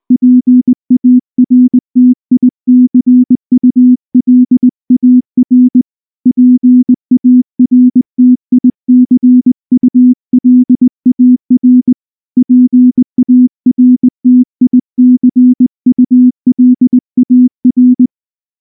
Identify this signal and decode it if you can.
Morse code, heard as a keyed tone, about 19 wpm, 250 Hz. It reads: PARTICULAR PARTICULAR PARTICULAR
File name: morse-training-20WPM